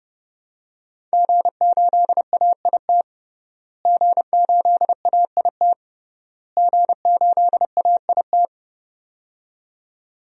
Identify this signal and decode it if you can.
Morse code, heard as a keyed tone, about 32 wpm, 700 Hz. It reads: G8AIT G8AIT G8AIT
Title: DXCC-100 Call Signs - 5 Characters - Repeated 3x - 30wpm